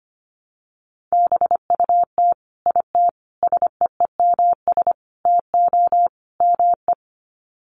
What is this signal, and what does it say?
Morse code, heard as a keyed tone, about 25 wpm, 700 Hz. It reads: BUT IT SEEMS TO ME
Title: Sets of 5 Words Encore2 25wpm